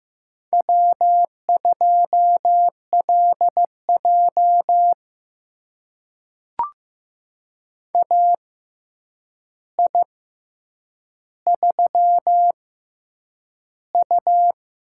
Morse code, heard as a keyed tone, about 15 wpm, spaced as for 8 wpm, 700 Hz. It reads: W2LJ AI3U
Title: USA Call Signs - 4 Characters - 6x Character Spacing - 15wpm